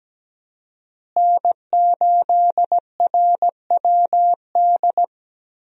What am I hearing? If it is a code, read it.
Morse code, heard as a keyed tone, about 17 wpm, 700 Hz. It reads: N8RWD
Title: USA Call Signs - 5 Characters - Repeated 2x - 17wpm